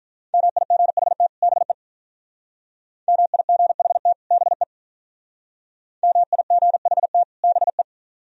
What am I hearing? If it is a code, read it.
Morse code, heard as a keyed tone, about 46 wpm, 700 Hz. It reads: MIGHT BE MIGHT BE MIGHT BE